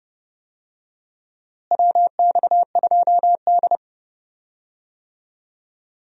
Morse code, heard as a keyed tone, about 30 wpm, 700 Hz. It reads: WX2D
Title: USA Call Signs - 4 Characters - Repeated 2x - 30wpm